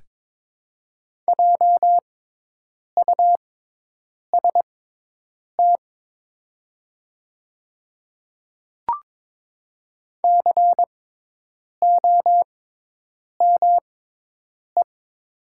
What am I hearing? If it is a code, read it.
Morse code, heard as a keyed tone, about 23 wpm, spaced as for 8 wpm, 700 Hz. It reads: JUST COME
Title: 6x Character Spacing - Top 300 Words - 22wpm